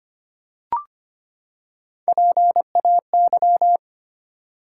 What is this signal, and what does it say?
Morse code, heard as a keyed tone, about 25 wpm, 700 Hz. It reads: PAY
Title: Three Letter Words - 25wpm